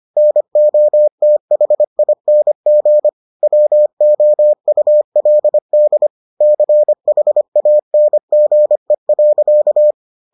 Morse code, heard as a keyed tone, about 25 wpm, 600 Hz. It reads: NOTHING WOULD CHANGE.